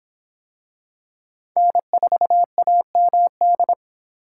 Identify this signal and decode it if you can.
Morse code, heard as a keyed tone, about 26 wpm, 700 Hz. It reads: N4AMD